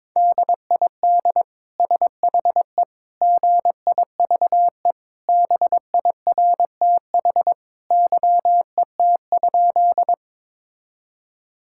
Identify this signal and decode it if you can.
Morse code, heard as a keyed tone, about 22 wpm, 700 Hz. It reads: DID SHE GIVE BIRTH YET?
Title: Sentences from Top 1000 Words - Review Part 5 - 22wpm